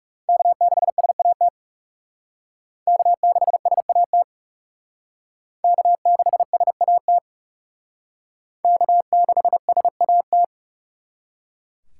Morse code, heard as a keyed tone, about 36 wpm, 700 Hz. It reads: K6SAT K6SAT K6SAT K6SAT
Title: Speed Racing - USA 1x3 Call Signs 30wpm